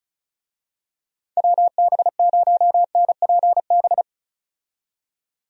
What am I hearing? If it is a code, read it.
Morse code, heard as a keyed tone, about 36 wpm, 700 Hz. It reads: WB0NPB
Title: USA Call Signs - 6 Characters - 35wpm